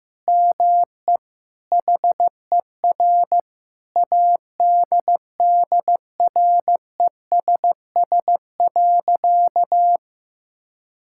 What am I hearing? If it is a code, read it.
Morse code, heard as a keyed tone, about 15 wpm, 700 Hz. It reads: ME HER ADDRESS.
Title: Sentences from Top 1000 Words - Review Part 6 - 15wpm